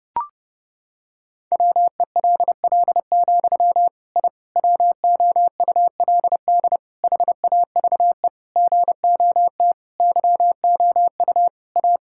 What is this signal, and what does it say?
Morse code, heard as a keyed tone, about 32 wpm, 700 Hz. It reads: WELL, I WOULD HAVE GOT YOU A
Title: Sentences from Top 1000 Words - Review Part 5 - 30wpm